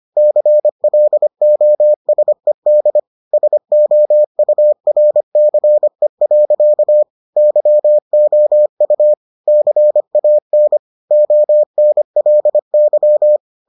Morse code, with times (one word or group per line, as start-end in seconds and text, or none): 0.2-3.0 CLOSED
3.3-7.0 SOURCE.
7.4-9.1 YOU
9.5-10.8 CAN
11.1-13.4 ONLY